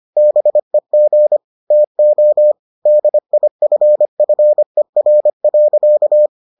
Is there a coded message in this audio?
Yes: Morse code from 0.2 to 6.3 s, heard as a keyed tone, about 25 wpm, 600 Hz. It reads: BEG TO DIFFER.